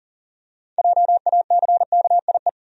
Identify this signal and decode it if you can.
Morse code, heard as a keyed tone, about 40 wpm, 700 Hz. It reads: JACKIE
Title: USA Common Names - 40wpm